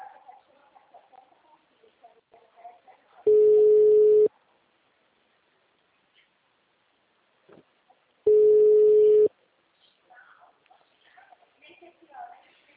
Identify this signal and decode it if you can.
Ringback tone (Europe, 425 Hz), with speech.